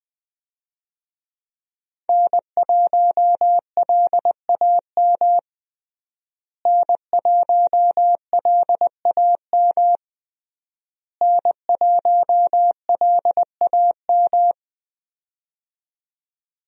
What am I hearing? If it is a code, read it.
Morse code, heard as a keyed tone, about 21 wpm, 700 Hz. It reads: N1LAM N1LAM N1LAM
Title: USA Call Signs - 5 Characters - Repeated 3x - 20wpm